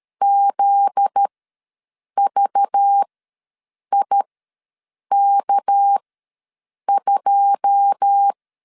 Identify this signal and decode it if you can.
Morse code, heard as a keyed tone, about 13 wpm, spaced as for 7 wpm, 800 Hz. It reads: ZVIK2
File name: LEZIONI / 9 2 ws 12wpm 3406